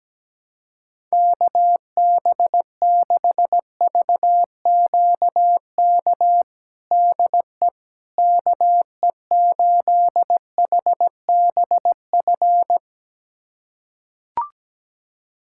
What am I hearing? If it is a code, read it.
Morse code, heard as a keyed tone, about 17 wpm, 700 Hz. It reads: KB6VQK DE KE8HBF
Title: USA Callsign DE Callsign - 6 Characters - 17wpm